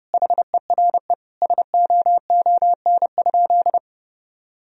Morse code, heard as a keyed tone, about 30 wpm, 700 Hz. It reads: HERE SOON?
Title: Sentences from Top 1000 Words - Review Part 3 - 30wpm